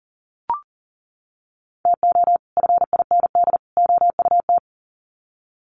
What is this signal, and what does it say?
Morse code, heard as a keyed tone, about 40 wpm, 700 Hz. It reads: TO FIND OUT
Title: Sets of 3 Words 40wpm